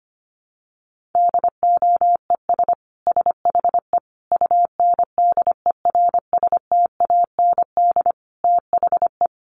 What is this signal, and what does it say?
Morse code, heard as a keyed tone, about 25 wpm, 700 Hz. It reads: DOES SHE UNDERSTAND THE
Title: Sentences from Top 1100 Words 25wpm